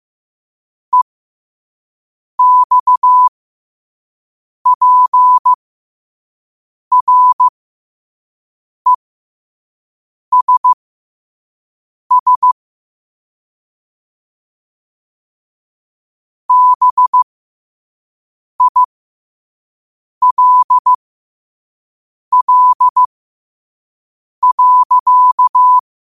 Morse code, heard as a keyed tone, about 15 wpm, spaced as for 5 wpm, 1000 Hz. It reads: EXPRESS BILL.
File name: qotdfiles / 5wpm